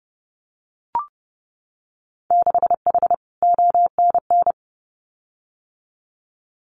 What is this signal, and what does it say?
Morse code, heard as a keyed tone, about 30 wpm, 700 Hz. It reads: 6H ONN
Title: Winter Field Day Exchanges Encore 30wpm